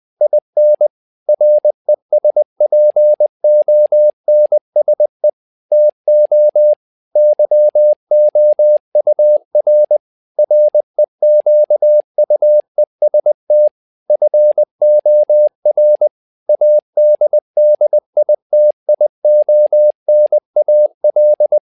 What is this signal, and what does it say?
Morse code, heard as a keyed tone, about 20 wpm, 600 Hz. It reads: IN RESPONSE TO YOUR REQUEST FOR ADDITIONAL
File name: tower trouble 20wpm